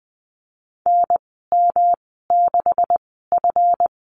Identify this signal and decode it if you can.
Morse code, heard as a keyed tone, about 20 wpm, spaced as for 15 wpm, 700 Hz. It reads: NM6F
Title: 200 Lesson 32g - Letter k - Call Signs 20wpm